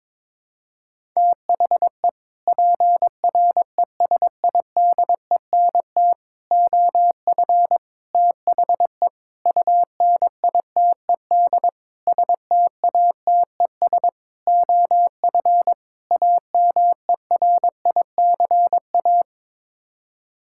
Morse code, heard as a keyed tone, about 22 wpm, 700 Hz. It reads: THE PRESIDENT OF THE UNITED STATES OF AMERICA
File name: practice / 22wpm / Sets of 8 Words - 22wpm